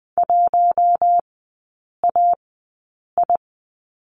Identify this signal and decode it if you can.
Morse code, heard as a keyed tone, about 20 wpm, spaced as for 8 wpm, 700 Hz. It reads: 1AI
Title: 047 Lesson 9e - Number 4 - Three Characters 20wpm